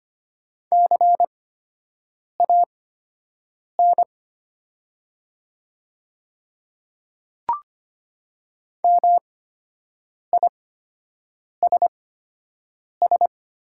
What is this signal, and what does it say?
Morse code, heard as a keyed tone, about 25 wpm, spaced as for 7 wpm, 700 Hz. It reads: CAN MISS
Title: 8x Character Spacing - Top 500 Words - 25wpm